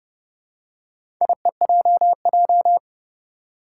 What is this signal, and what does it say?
Morse code, heard as a keyed tone, about 30 wpm, 690 Hz. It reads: IEJJ
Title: Four Challenging Letter-Number Combinations 30wpm